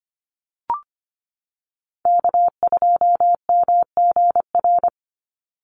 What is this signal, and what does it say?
Morse code, heard as a keyed tone, about 25 wpm, 700 Hz. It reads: K2MGR
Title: Call Signs - 25wpm